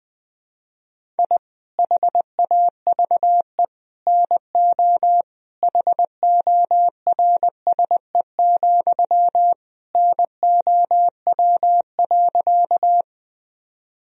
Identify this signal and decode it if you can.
Morse code, heard as a keyed tone, about 20 wpm, 700 Hz. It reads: I HAVE NO HORSE, NOW.
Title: Sentences from Top 1000 Words - Review Part 1 - 20wpm